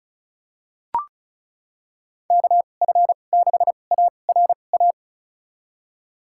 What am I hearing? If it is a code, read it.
Morse code, heard as a keyed tone, about 36 wpm, spaced as for 25 wpm, 700 Hz. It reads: KF6ARA
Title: USA Call Signs - 6 Characters - 2x Character Spacing - 35wpm